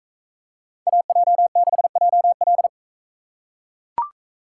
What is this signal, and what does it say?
Morse code, heard as a keyed tone, about 40 wpm, 700 Hz. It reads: AJ6JL